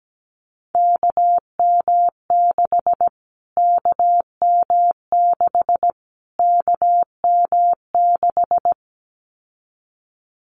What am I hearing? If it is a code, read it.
Morse code, heard as a keyed tone, about 17 wpm, 700 Hz. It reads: KM6 KM6 KM6
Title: Callsign Prefixes for Top 25 DXCC - Repeated 3x - 17wpm